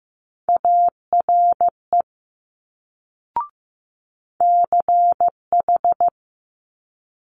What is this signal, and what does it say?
Morse code, heard as a keyed tone, about 15 wpm, 700 Hz. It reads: ARE CH
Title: Multi-letter Phonemes Quick Fire 15wpm